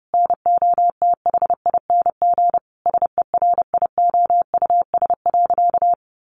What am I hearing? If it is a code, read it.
Morse code, heard as a keyed tone, about 30 wpm, 700 Hz. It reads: NOTHING SERIOUS.
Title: Sentences from Top 1100 Words 30wpm